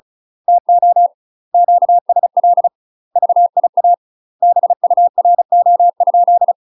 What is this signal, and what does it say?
Morse code, heard as a keyed tone, about 36 wpm, 700 Hz. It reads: TO QSL VIA BURO?
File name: UK MASTER QSOs TRAINING / 35WPM / UK MASTER QSOs 0010